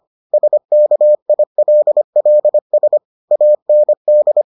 Morse code, heard as a keyed tone, about 25 wpm, 600 Hz. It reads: SKILLS AND